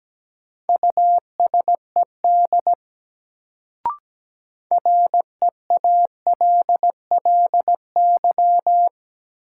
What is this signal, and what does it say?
Morse code, heard as a keyed tone, about 17 wpm, 700 Hz. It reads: USED REALLY
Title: Sentences from Top 600 Words - New Word Drill - 17wpm